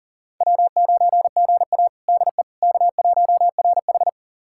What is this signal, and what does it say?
Morse code, heard as a keyed tone, about 40 wpm, 700 Hz. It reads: W9GA DE K1RH
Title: USA Callsign DE Callsign - 4 Characters - 40wpm